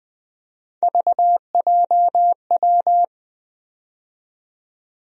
Morse code, heard as a keyed tone, about 21 wpm, 700 Hz. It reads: VJW
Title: USA Call Sign Postfixes - 3 Letters 20wpm